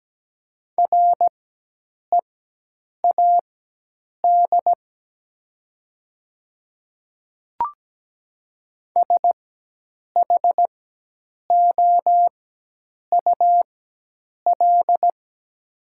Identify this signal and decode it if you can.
Morse code, heard as a keyed tone, about 17 wpm, spaced as for 8 wpm, 700 Hz. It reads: READ SHOUL
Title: 4x Character Spacing - Top 500 Words - 17wpm